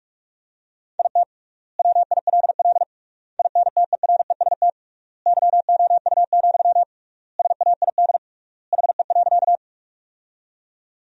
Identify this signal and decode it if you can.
Morse code, heard as a keyed tone, about 42 wpm, 700 Hz. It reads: IT WILL INTEREST YOU, SAID HE.
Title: 3x Word-Spacing - Sentences from Top 400 Words - 45wpm